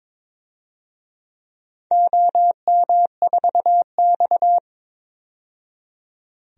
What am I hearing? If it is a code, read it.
Morse code, heard as a keyed tone, about 23 wpm, 700 Hz. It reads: OM4X
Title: DXCC-100 Call Signs - 4 Characters - 22wpm